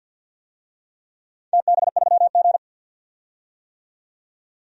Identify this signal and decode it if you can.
Morse code, heard as a keyed tone, about 47 wpm, 700 Hz. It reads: TB3G
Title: DXCC-100 Call Signs - 4 Characters - 50wpm